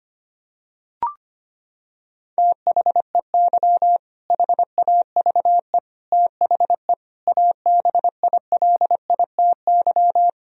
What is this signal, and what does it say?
Morse code, heard as a keyed tone, about 25 wpm, 700 Hz. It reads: THEY HAVE THE ABILITY
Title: Sentences from Top 1000 Words - Review Part 3 - 25wpm